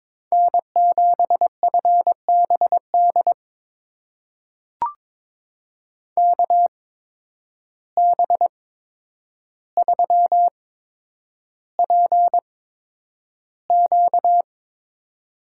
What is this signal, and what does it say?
Morse code, heard as a keyed tone, about 22 wpm, spaced as for 9 wpm, 700 Hz. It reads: N7FBD KB3PQ
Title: USA Call Signs - 5 Characters - 8x Character Spacing - 22wpm